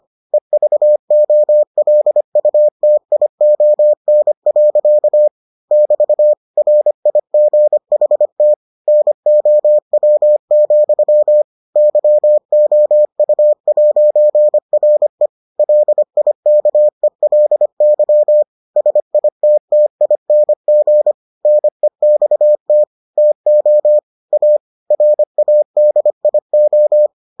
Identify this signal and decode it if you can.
Morse code, heard as a keyed tone, about 25 wpm, 600 Hz. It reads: EVOLUTION. <BT> RIGHT NOW, YOU'RE LIKELY SITTING NEXT TO A RADIO